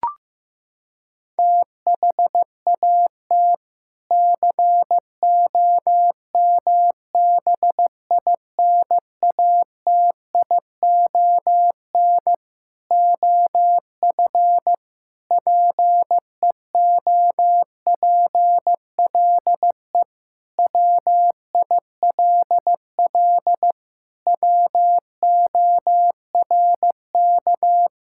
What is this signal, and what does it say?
Morse code, heard as a keyed tone, about 15 wpm, 700 Hz. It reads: THAT COMBINATION OF PEOPLE WILL WORK
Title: Sentences from Top 1000 Words - Review Part 6 - 15wpm